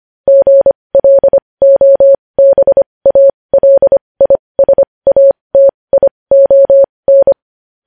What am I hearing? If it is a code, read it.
Morse code, heard as a keyed tone, about 25 wpm, spaced as for 20 wpm, 550 Hz. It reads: GLOBALISATION
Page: news headlines in morse code at 20 wpm